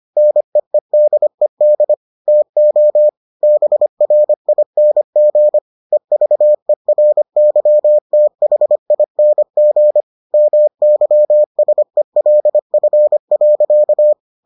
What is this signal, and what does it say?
Morse code, heard as a keyed tone, about 25 wpm, 600 Hz. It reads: NEEDED TO BRING EVERYTHING MYSELF.